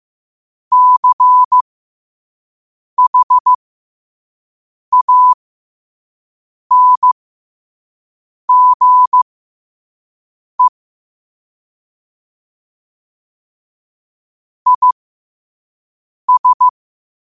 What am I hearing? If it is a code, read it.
Morse code, heard as a keyed tone, about 15 wpm, spaced as for 5 wpm, 1000 Hz. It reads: CHANGE IS
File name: qotdfiles / 5wpm